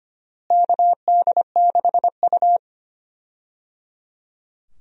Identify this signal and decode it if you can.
Morse code, heard as a keyed tone, about 25 wpm, 700 Hz. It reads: KD6U
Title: USA Call Signs - 4 Characters - Repeated 3x - 25wpm